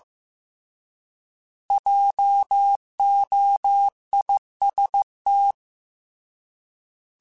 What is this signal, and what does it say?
Morse code, heard as a keyed tone, about 15 wpm, 800 Hz. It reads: JOIST
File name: Alphabetical / Alphabetical mp3s / Js @13